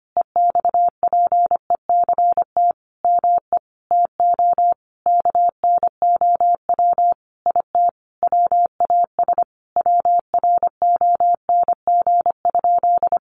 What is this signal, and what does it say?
Morse code, heard as a keyed tone, about 25 wpm, 700 Hz. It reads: EXPECT ME TO KNOW IT WAS WRONG?
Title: Sentences from Top 1100 Words 25wpm